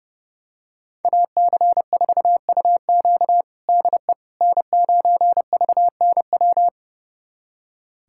Morse code, heard as a keyed tone, about 30 wpm, 700 Hz. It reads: AC4UQ DE N9VNW
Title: USA Callsign DE Callsign - 5 Characters - 30wpm